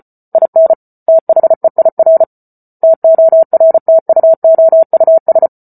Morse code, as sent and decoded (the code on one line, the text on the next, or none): .. -. / - .... .. .-. / - --- .-. - ..- --- ..- ...
IN THEIR TORTUOUS